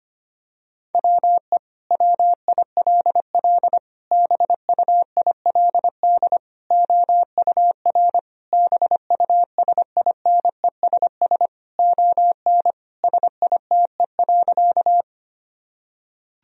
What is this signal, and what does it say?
Morse code, heard as a keyed tone, about 25 wpm, 700 Hz. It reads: WE WILL BUILD OUR BUSINESS ON SITE.
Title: Sentences from Top 1000 Words - Review Part 4 - 25wpm